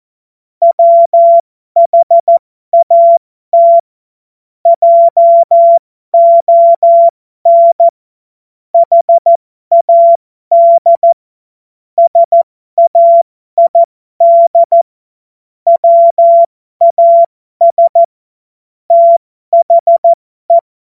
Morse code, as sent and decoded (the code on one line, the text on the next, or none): .-- .... .- - / .--- --- -. / .... .- -.. / ... .- .. -.. / .-- .- ... / - .... .
WHAT JON HAD SAID WAS THE